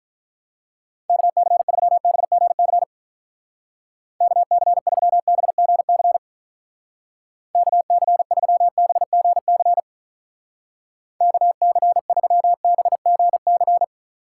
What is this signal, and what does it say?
Morse code, heard as a keyed tone, about 44 wpm, 700 Hz. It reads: KC3BGC KC3BGC KC3BGC KC3BGC